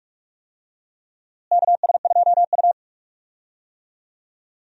Morse code, heard as a keyed tone, about 43 wpm, 700 Hz. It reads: KS2U
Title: USA Call Signs - 4 Characters - Repeated 2x - 45wpm